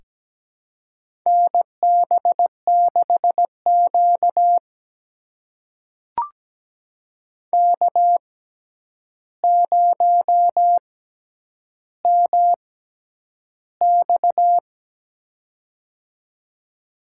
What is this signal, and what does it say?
Morse code, heard as a keyed tone, about 17 wpm, spaced as for 9 wpm, 700 Hz. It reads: NB6Q K0MX
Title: USA Call Signs - 4 Characters - 6x Character Spacing - 17wpm